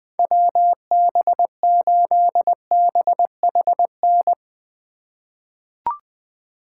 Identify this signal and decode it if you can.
Morse code, heard as a keyed tone, about 20 wpm, 700 Hz. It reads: WB8BHN
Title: 2023 SOTA Chaser Call Signs - 20wpm